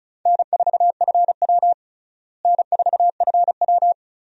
Morse code, heard as a keyed tone, about 34 wpm, 700 Hz. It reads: N4FW N4FW
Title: USA Call Signs - 4 Characters - Repeated 3x - 35wpm